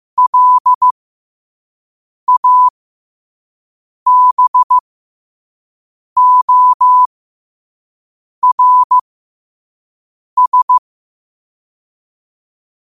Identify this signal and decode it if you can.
Morse code, heard as a keyed tone, about 15 wpm, spaced as for 5 wpm, 1000 Hz. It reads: LABORS